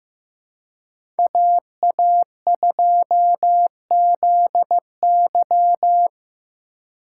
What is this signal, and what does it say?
Morse code, heard as a keyed tone, about 15 wpm, 700 Hz. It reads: AA2ZY
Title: USA Call Signs - 5 Characters - 4x Character Spacing - 15wpm